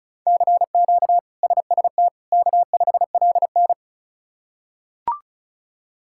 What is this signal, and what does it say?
Morse code, heard as a keyed tone, about 36 wpm, 700 Hz. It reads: CQ SST K5LN
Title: K1USN SST QSO Practice - 35wpm